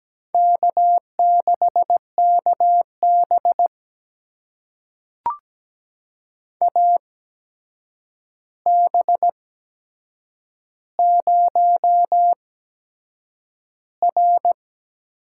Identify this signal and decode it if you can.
Morse code, heard as a keyed tone, about 17 wpm, spaced as for 7 wpm, 700 Hz. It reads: K6KB AB0R
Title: USA Call Signs - 8x Character Spacing - 17wpm